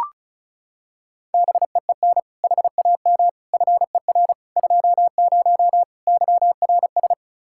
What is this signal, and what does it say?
Morse code, heard as a keyed tone, about 36 wpm, 700 Hz. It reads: BEEN HAM FER 20 YRS
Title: Practice QSO - Exchange 3 - More Info - 35wpm